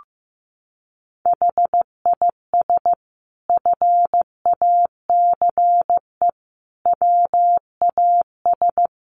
Morse code, heard as a keyed tone, about 15 wpm, 700 Hz. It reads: HIS FACE WAS
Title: A Sound Of Thunder 15wpm